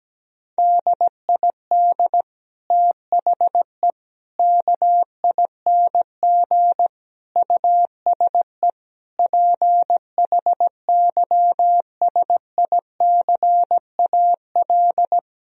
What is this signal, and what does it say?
Morse code, heard as a keyed tone, about 17 wpm, 700 Hz. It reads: DID THE KING USE PHYSICAL
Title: Sentences from Top 1000 Words - Review Part 4 - 17wpm